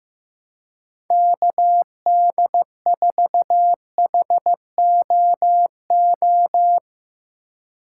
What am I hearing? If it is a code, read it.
Morse code, heard as a keyed tone, about 15 wpm, 700 Hz. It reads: KD4HOO
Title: USA Call Signs - 6 Characters - 3x Character Spacing - 15wpm